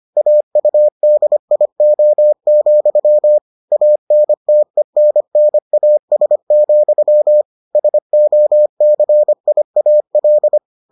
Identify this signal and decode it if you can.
Morse code, heard as a keyed tone, about 25 wpm, 600 Hz. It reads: AUDIO, ANTENNAS, SOCIAL